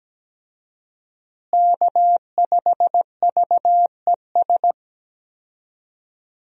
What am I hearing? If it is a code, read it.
Morse code, heard as a keyed tone, about 17 wpm, 700 Hz. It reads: K5VES
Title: USA Call Signs - 5 Characters - Repeated 3x - 17wpm